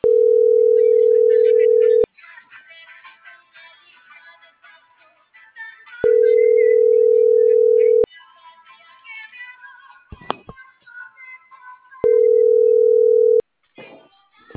Ringback tone (North America), 440 + 480 Hz.